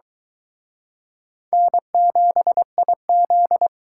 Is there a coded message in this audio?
Yes: Morse code from 1.5 to 3.7 s, heard as a keyed tone, about 23 wpm, 700 Hz. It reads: N7IZ